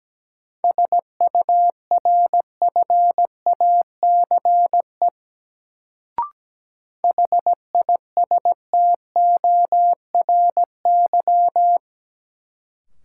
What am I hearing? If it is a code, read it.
Morse code, heard as a keyed tone, about 17 wpm, 700 Hz. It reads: SURFACE HISTORY
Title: Sentences from Top 600 Words - New Word Drill - 17wpm